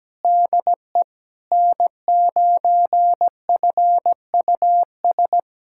Morse code, heard as a keyed tone, about 17 wpm, 700 Hz. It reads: DE N9FUS
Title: USA Callsign DE Callsign - 5 Characters - 17wpm